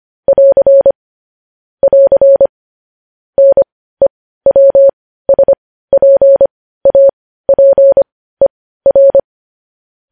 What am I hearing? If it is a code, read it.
Morse code, heard as a keyed tone, about 25 wpm, spaced as for 15 wpm, 550 Hz. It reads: <AR> <AR> NEWSPAPER
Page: news-headlines-in-morse-code-at-15-wpm